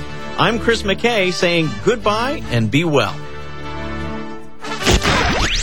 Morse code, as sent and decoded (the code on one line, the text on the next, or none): none